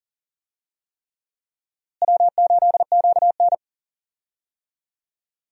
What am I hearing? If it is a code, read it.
Morse code, heard as a keyed tone, about 40 wpm, 700 Hz. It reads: W8QN
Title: USA Call Signs - 4 Characters - Repeated 3x - 40wpm